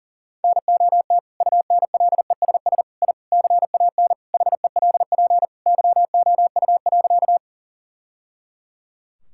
Morse code, heard as a keyed tone, about 40 wpm, 700 Hz. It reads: NOT UNLESS I CAN HELP YOU.
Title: Sentences from Top 1000 Words - Review Part 6 - 40wpm